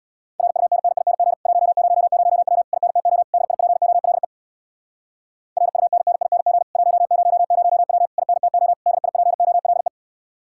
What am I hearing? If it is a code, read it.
Morse code, heard as a keyed tone, about 67 wpm, 700 Hz. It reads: RUNNING 100W INTO DIPOLE RUNNING 100W INTO DIPOLE